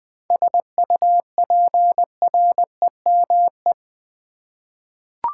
Morse code, heard as a keyed tone, about 20 wpm, 700 Hz. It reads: SUPREME
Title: 7 Letter Words - First 3 Letters Given - 20wpm